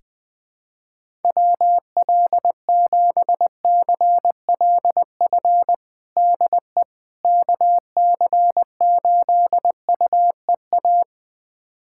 Morse code, heard as a keyed tone, about 20 wpm, 700 Hz. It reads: WL7CLF DE KC8UEA
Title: USA Callsign DE Callsign - 6 Characters - 20wpm